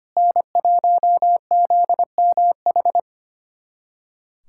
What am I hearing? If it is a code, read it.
Morse code, heard as a keyed tone, about 25 wpm, 700 Hz. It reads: N1ZMH
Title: USA Callsign DE Callsign - 5 Characters - 25wpm